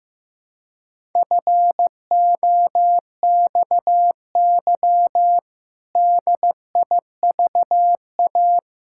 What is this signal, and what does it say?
Morse code, heard as a keyed tone, about 15 wpm, 700 Hz. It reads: FOXY DIVA
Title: Pangram Challenge v2 - 15wpm